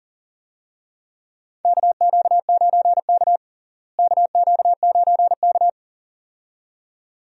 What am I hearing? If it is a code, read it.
Morse code, heard as a keyed tone, about 40 wpm, 700 Hz. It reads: KQ9K KQ9K
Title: USA Call Signs - 4 Characters - Repeated 2x - 40wpm